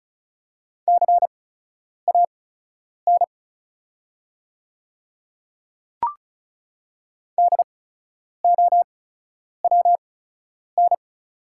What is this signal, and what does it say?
Morse code, heard as a keyed tone, about 36 wpm, spaced as for 10 wpm, 700 Hz. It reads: CAN DOWN
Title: 8x Character Spacing - Top 200 Words - 35wpm